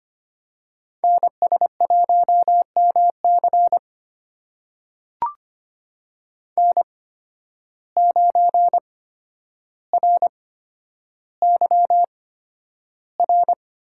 Morse code, heard as a keyed tone, about 25 wpm, spaced as for 11 wpm, 700 Hz. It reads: NS1MC N9RYR
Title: USA Call Signs - 8x Character Spacing - 25wpm